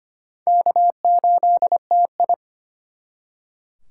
Morse code, heard as a keyed tone, about 32 wpm, 700 Hz. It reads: K8TI